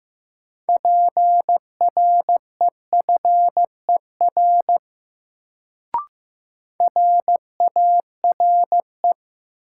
Morse code, heard as a keyed tone, about 16 wpm, 700 Hz. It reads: PREFER RARE
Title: Sentences from Top 1100 Words - New Word Drill - 15wpm